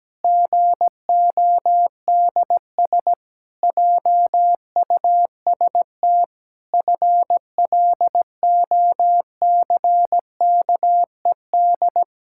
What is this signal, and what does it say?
Morse code, heard as a keyed tone, about 17 wpm, 700 Hz. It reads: GODS JUST FLOCKED
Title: Pangram Challenge v2 - 17wpm